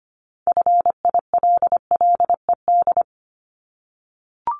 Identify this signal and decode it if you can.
Morse code, heard as a keyed tone, about 25 wpm, 700 Hz. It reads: FILLED
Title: Multi-letter Phonemes 25wpm